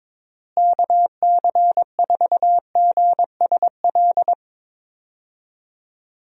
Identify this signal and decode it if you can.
Morse code, heard as a keyed tone, about 22 wpm, 700 Hz. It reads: KC4GSL
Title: USA Call Signs - 6 Characters - Repeated 2x - 22wpm